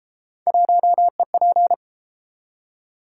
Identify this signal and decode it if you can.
Morse code, heard as a keyed tone, about 31 wpm, 700 Hz. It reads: 1EP